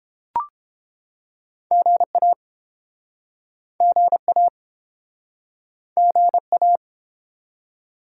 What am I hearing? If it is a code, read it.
Morse code, heard as a keyed tone, about 33 wpm, 700 Hz. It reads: GA GA GA